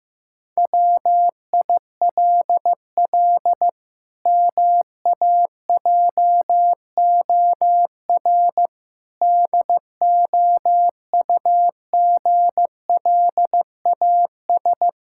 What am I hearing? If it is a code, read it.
Morse code, heard as a keyed tone, about 15 wpm, 700 Hz. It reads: WILL MAJOR DOUGLAS
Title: Pangram Challenge v2 - 15wpm